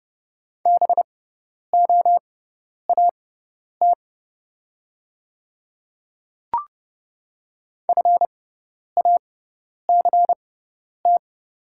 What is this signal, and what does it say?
Morse code, heard as a keyed tone, about 30 wpm, spaced as for 10 wpm, 700 Hz. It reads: BOAT FACT
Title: 6x Character Spacing - Top 500 Words - 30wpm